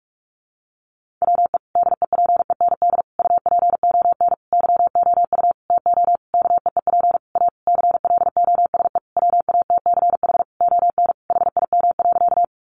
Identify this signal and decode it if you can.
Morse code, heard as a keyed tone, about 44 wpm, 700 Hz. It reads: WE DEPEND UPON YOU TO KEEP A CLOSE WATCH ON HIM.
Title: Sentences from Top 1300 Words 45wpm